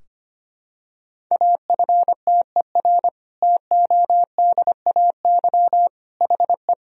Morse code, heard as a keyed tone, about 25 wpm, 700 Hz. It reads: AFTER TODAY HE
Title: Sentences from Top 1000 Words - Review Part 4 - 25wpm